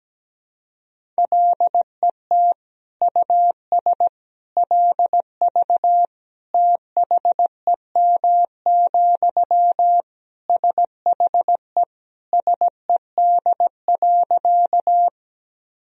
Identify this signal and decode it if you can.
Morse code, heard as a keyed tone, about 17 wpm, 700 Hz. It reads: LET US LV THEM, SHE SED.
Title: Sentences from Top 400 Words and Abbreviations 17wpm